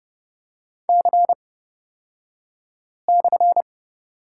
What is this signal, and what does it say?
Morse code, heard as a keyed tone, about 30 wpm, spaced as for 5 wpm, 700 Hz. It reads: C/
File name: Lesson 36d - BT - Two Characters 30wpm